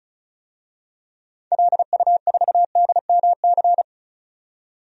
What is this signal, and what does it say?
Morse code, heard as a keyed tone, about 36 wpm, 700 Hz. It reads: LU4DMC
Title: DXCC-100 Call Signs - 6 Characters - 35wpm